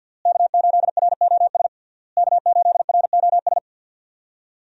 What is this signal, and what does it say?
Morse code, heard as a keyed tone, about 48 wpm, 700 Hz. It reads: K8ROS K8ROS
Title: USA Call Signs - 5 Characters - Repeated 3x - 50wpm